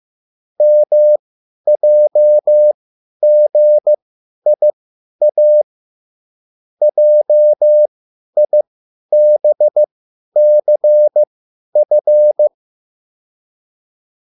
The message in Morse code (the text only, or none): MJGIA JIBCF